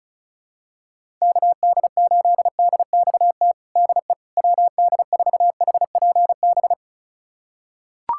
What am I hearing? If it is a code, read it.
Morse code, heard as a keyed tone, about 36 wpm, 700 Hz. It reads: KD8DXT DE WD4HPB
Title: USA Callsign DE Callsign - 6 Characters - 35wpm